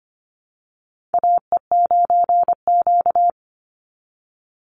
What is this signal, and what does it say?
Morse code, heard as a keyed tone, about 25 wpm, 700 Hz. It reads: AE9Q
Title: NA SOTA Chaser Callsigns 25wpm